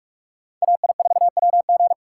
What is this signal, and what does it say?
Morse code, heard as a keyed tone, about 45 wpm, 700 Hz. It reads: AI4WG